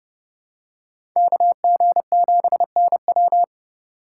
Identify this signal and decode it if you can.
Morse code, heard as a keyed tone, about 31 wpm, 700 Hz. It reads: KG7NW
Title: USA Call Signs - 5 Characters - 6x Character Spacing - 30wpm